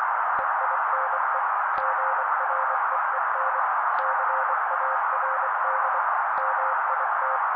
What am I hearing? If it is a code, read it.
Morse code, heard as a keyed tone, about 24 wpm, 530 Hz. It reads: EINE GREEN CARD MIT
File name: daily / 23wpm noise